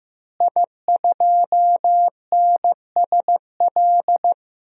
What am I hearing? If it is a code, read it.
Morse code, heard as a keyed tone, about 16 wpm, 700 Hz. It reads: I2NSL
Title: DXCC-100 Call Signs - 5 Characters - Repeated 2x - 15wpm